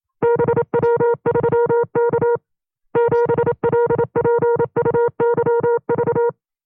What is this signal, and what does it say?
Morse code, heard as a keyed tone, about 27 wpm, 470 Hz. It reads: BW3K 7LPUYV
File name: FreeFiles / CW-practice / 25WPM-side1